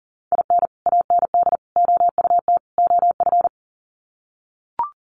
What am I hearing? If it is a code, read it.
Morse code, heard as a keyed tone, about 40 wpm, 700 Hz. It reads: IN AND OUT OF
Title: Sets of 4 Words 40wpm